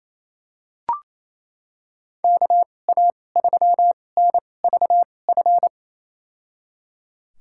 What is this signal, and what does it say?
Morse code, heard as a keyed tone, about 27 wpm, spaced as for 20 wpm, 700 Hz. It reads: KA3NVF
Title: USA Call Signs - 6 Characters - 2x Character Spacing - 28wpm